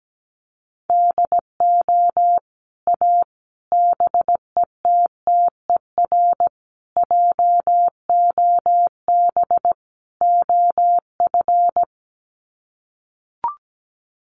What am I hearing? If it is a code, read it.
Morse code, heard as a keyed tone, about 17 wpm, 700 Hz. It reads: DO A BETTER JOB OF
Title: Sets of 5 Words Encore2 17wpm